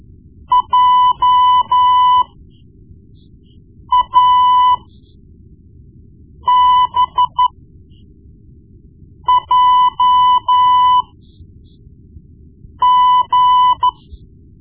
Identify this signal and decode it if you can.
Morse code, heard as a keyed tone, about 13 wpm, spaced as for 4 wpm, 1000 Hz. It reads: JABJG
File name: Morse Lessons / Morse Lessons / Tape-1B